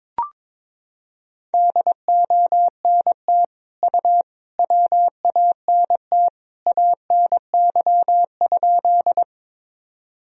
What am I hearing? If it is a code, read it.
Morse code, heard as a keyed tone, about 22 wpm, 700 Hz. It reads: DONT U WANT ANY?
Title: Sentences from Top 300 Words and Abbreviations 22wpm